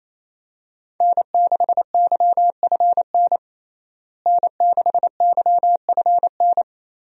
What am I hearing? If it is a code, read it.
Morse code, heard as a keyed tone, about 28 wpm, 700 Hz. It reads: N6YFN N6YFN
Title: USA Call Signs - 5 Characters - Repeated 3x - 28wpm